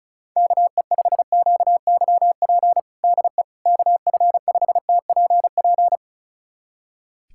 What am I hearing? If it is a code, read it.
Morse code, heard as a keyed tone, about 36 wpm, 700 Hz. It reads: KE5QYP DE KF5TPP
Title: USA Callsign DE Callsign - 6 Characters - 35wpm